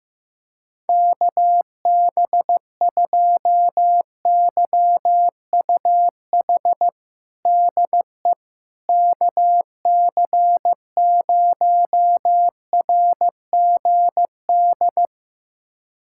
Morse code, heard as a keyed tone, about 15 wpm, 700 Hz. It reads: KB2YUH DE KC0RGD
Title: USA Callsign DE Callsign - 6 Characters - 15wpm